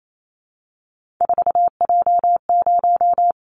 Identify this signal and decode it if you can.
Morse code, heard as a keyed tone, about 28 wpm, 700 Hz. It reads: 4J0
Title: Callsign Prefixes for Top 100 DXCC - 28wpm